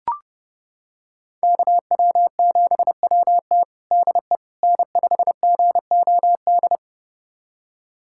Morse code, heard as a keyed tone, about 32 wpm, 700 Hz. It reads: KW7WT DE N5GOD
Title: USA Callsign DE Callsign - 5 Characters - 30wpm